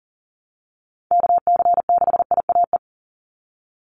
Morse code, heard as a keyed tone, about 40 wpm, 700 Hz. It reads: KC6IAE
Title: USA Call Signs 40wpm